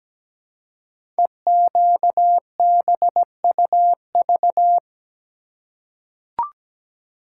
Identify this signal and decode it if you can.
Morse code, heard as a keyed tone, about 17 wpm, 700 Hz. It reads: EQBUV
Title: Code Groups - 5 Letters - 17wpm